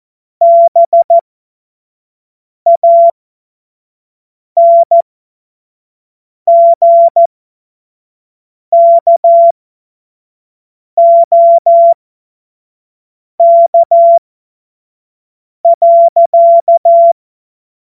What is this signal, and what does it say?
Morse code, heard as a keyed tone, about 14 wpm, spaced as for 5 wpm, 680 Hz. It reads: BANGKOK.
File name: Morsekurs / Witze / Witz 07. Februar 2023 25 F70